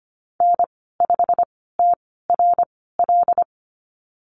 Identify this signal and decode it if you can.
Morse code, heard as a keyed tone, about 25 wpm, spaced as for 16 wpm, 700 Hz. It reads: N5TRL
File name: Lesson 15g - Number 5 - Call Signs 25wpm